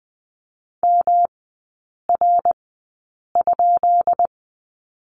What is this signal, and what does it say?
Morse code, heard as a keyed tone, about 20 wpm, spaced as for 8 wpm, 690 Hz. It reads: MR?
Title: 133 Lesson 22e - Question Mark - Three Characters 20wpm